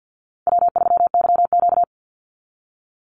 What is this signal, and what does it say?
Morse code, heard as a keyed tone, about 49 wpm, 700 Hz. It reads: W3YQ
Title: Easy Call Signs Repeated 2x - 50wpm